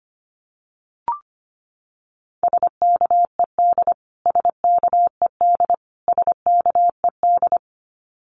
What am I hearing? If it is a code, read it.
Morse code, heard as a keyed tone, about 25 wpm, 700 Hz. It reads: SKED SKED SKED
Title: Abbreviations Repeated 3x - 25wpm